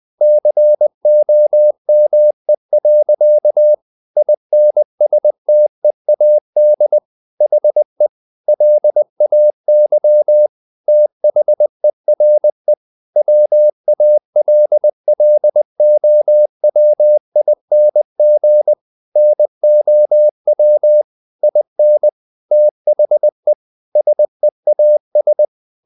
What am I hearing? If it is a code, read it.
Morse code, heard as a keyed tone, about 20 wpm, 600 Hz. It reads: COME. INSTEAD HE LAY THERE WALLOWING NOW IN THE SEAS
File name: the old man and the sea part 2 20wpm